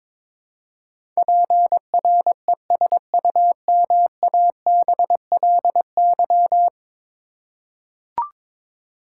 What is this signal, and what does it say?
Morse code, heard as a keyed tone, about 22 wpm, 700 Hz. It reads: PRESUMABLY
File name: practice / 22wpm / 10 Letter Words - First 3 Letters Given - 22wpm